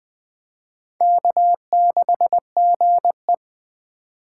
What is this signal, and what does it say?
Morse code, heard as a keyed tone, about 20 wpm, 700 Hz. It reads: K6GE
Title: USA Call Signs - 4 Characters - 3x Character Spacing - 20wpm